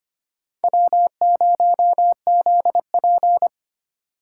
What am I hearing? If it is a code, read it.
Morse code, heard as a keyed tone, about 25 wpm, 700 Hz. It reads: W0ZP
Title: K1USN SST QSO Practice - 25wpm